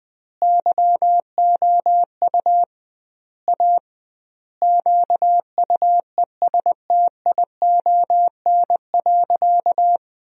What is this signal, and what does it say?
Morse code, heard as a keyed tone, about 20 wpm, 700 Hz. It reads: YOU A QUESTION.
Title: 2x Word-Spacing - Sentences from Top 400 Words - 20wpm